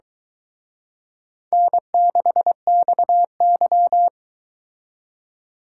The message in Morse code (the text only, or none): N6XY